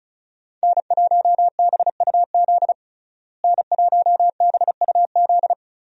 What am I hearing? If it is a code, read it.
Morse code, heard as a keyed tone, about 34 wpm, 700 Hz. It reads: N1BUZ N1BUZ
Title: USA Call Signs - 5 Characters - Repeated 2x - 35wpm